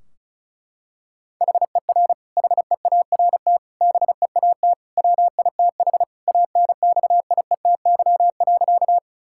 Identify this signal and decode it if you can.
Morse code, heard as a keyed tone, about 34 wpm, 700 Hz. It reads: HER HEART BEAT WITH ANXIETY.
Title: Sentences from Top 1600 Words 35wpm